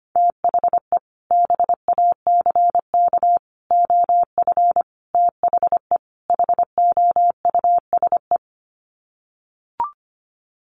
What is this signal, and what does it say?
Morse code, heard as a keyed tone, about 25 wpm, 700 Hz. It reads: THE BACK OF THE HOUSE
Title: Sets of 5 Words Encore2 25wpm